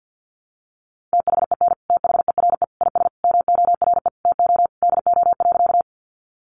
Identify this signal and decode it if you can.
Morse code, heard as a keyed tone, about 50 wpm, 700 Hz. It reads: THEN THERE IS MORE TO DO.
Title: Sentences from Top 100 Words 50wpm